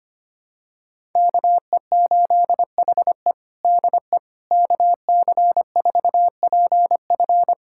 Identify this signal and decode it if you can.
Morse code, heard as a keyed tone, about 25 wpm, 700 Hz. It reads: KE8HE DE KC4PF
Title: USA Callsign DE Callsign - 5 Characters - 25wpm